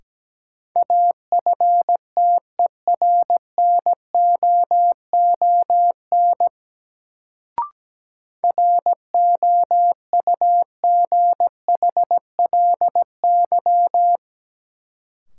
Morse code, heard as a keyed tone, about 17 wpm, 700 Hz. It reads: AFTERNOON ROUGHLY
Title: Sentences from Top 1500 Words - New Word Drill Alt - 17wpm